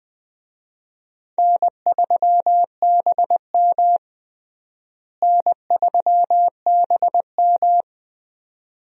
Morse code, heard as a keyed tone, about 20 wpm, 700 Hz. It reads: N3BM N3BM
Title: USA Call Signs - 4 Characters - Repeated 2x - 20wpm